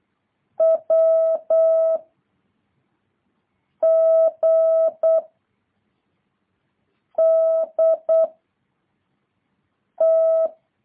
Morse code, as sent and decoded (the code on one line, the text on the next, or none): .-- / --. / -.. / -
W G D T